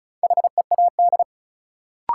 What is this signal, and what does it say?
Morse code, heard as a keyed tone, about 36 wpm, 700 Hz. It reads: HEAD
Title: Sentences from Top 300 Words - New Word Drill - 35wpm